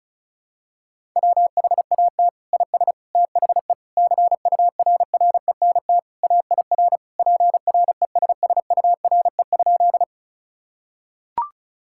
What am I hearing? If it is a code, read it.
Morse code, heard as a keyed tone, about 34 wpm, 700 Hz. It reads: WHAT IS THE CURRENT AIR PRESSURE?
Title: Sentences from Top 1000 Words - Review Part 6 - 35wpm